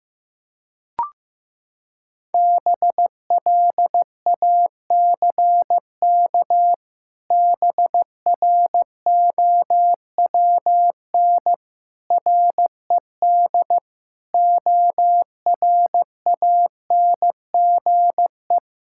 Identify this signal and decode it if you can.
Morse code, heard as a keyed tone, about 14 wpm, 700 Hz. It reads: BLACK BROWN RED ORANGE
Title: Pangram Challenge v2 - 15wpm